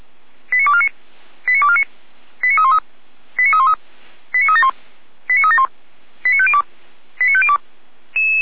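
Five-tone selective call (ZVEI), heard as quick runs of five tones, tones of about 70 ms. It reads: 89249 89249 89313 89313 89371 89371 89502 89502
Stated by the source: ZVEI five-tone selective call 89229, received by SDR.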